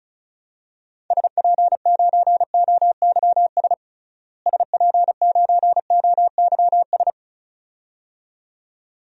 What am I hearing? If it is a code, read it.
Morse code, heard as a keyed tone, about 34 wpm, 700 Hz. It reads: SP9OYS SP9OYS
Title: DXCC-100 Call Signs - 6 Characters - Repeated 2x - 35wpm